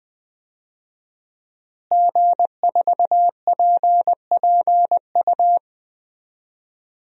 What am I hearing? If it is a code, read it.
Morse code, heard as a keyed tone, about 20 wpm, 700 Hz. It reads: G4PPU
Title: DXCC-100 Call Signs - 5 Characters - Repeated 2x - 20wpm